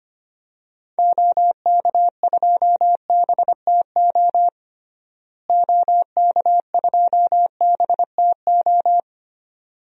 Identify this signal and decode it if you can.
Morse code, heard as a keyed tone, about 25 wpm, 700 Hz. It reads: OK2BTO OK2BTO
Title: DXCC-100 Call Signs - 6 Characters - Repeated 2x - 25wpm